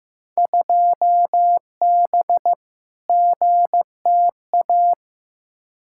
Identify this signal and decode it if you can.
Morse code, heard as a keyed tone, about 15 wpm, 700 Hz. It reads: 2B GTA
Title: Field Day Exchanges Encore - 15wpm